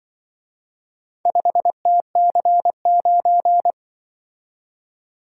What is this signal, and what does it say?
Morse code, heard as a keyed tone, about 24 wpm, 700 Hz. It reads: 5TC9
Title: Speed-Racing - Four Letter-Number Combinations - 20wpm